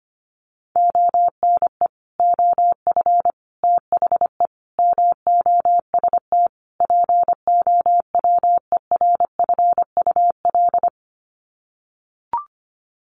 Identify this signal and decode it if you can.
Morse code, heard as a keyed tone, about 25 wpm, 700 Hz. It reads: ONE OF THE MOST POWERFUL
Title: Sets of 5 Words Encore 25wpm